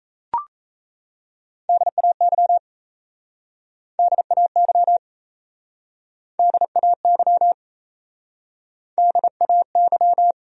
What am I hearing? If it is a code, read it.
Morse code, heard as a keyed tone, about 37 wpm, 700 Hz. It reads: DAY DAY DAY DAY